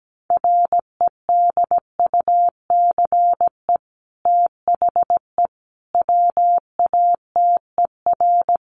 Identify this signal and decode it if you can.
Morse code, heard as a keyed tone, about 17 wpm, 700 Hz. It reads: REDUCE THE WATER
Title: Sentences from Top 1100 Words 17wpm